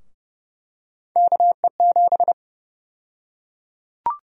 Morse code, heard as a keyed tone, about 30 wpm, 700 Hz. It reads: KE7
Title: USA Call Sign Prefixes - 30wpm